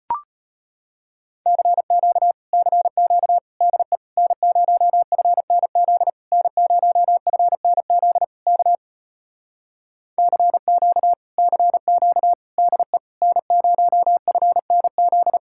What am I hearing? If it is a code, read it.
Morse code, heard as a keyed tone, about 33 wpm, 700 Hz. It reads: CQ CQ DE N0FNZ N0FNZ K CQ CQ DE N0FNZ